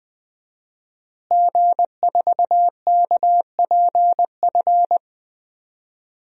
Morse code, heard as a keyed tone, about 20 wpm, 700 Hz. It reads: G4KPF
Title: DXCC-100 Call Signs - 5 Characters - Repeated 3x - 20wpm